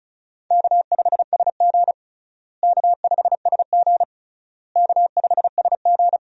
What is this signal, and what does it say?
Morse code, heard as a keyed tone, about 36 wpm, 700 Hz. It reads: K5SG K5SG K5SG
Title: USA Call Signs - 4 Characters - Repeated 3x - 35wpm